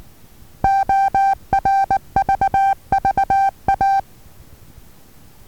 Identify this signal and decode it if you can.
Morse code, heard as a keyed tone, about 19 wpm, 780 Hz. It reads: ORVVA
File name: Taviro HA6VA B